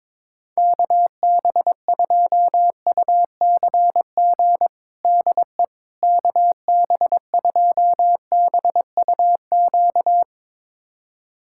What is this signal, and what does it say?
Morse code, heard as a keyed tone, about 22 wpm, 700 Hz. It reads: KB2UCG DE KB2BUQ
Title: USA Callsign DE Callsign - 6 Characters - 22wpm